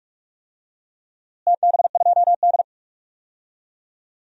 Morse code, heard as a keyed tone, about 44 wpm, 700 Hz. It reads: TB2D
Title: DXCC-100 Call Signs - 4 Characters - Repeated 3x - 45wpm